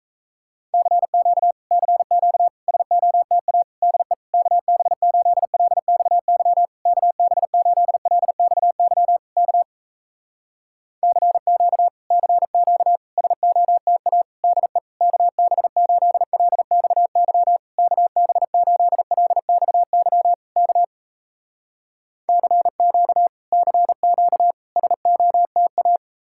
Morse code, heard as a keyed tone, about 38 wpm, 700 Hz. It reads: CQ CQ SOTA DE KB8LXY KB8LXY K CQ CQ SOTA DE KB8LXY KB8LXY K CQ CQ SOTA